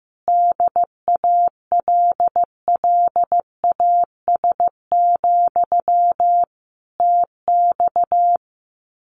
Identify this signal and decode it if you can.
Morse code, heard as a keyed tone, about 14 wpm, 700 Hz. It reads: DALLAS, TX
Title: Common QSO Phrases Encore 15wpm